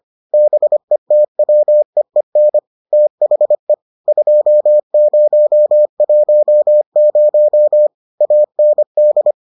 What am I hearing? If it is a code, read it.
Morse code, heard as a keyed tone, about 25 wpm, 600 Hz. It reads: BETWEEN THE 2010 AND